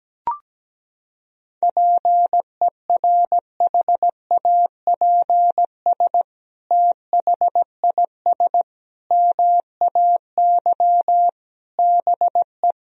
Morse code, heard as a keyed tone, about 17 wpm, 700 Hz. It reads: PERHAPS THIS MAY BE
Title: Sentences from Top 1000 Words - Review Part 2 - 17wpm